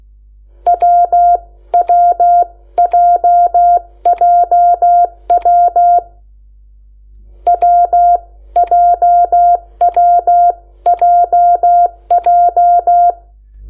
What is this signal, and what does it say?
Morse code, heard as a keyed tone, about 16 wpm, spaced as for 14 wpm, 700 Hz. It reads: WWJJW WJWJJ
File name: Lecciones / CLASE Nº8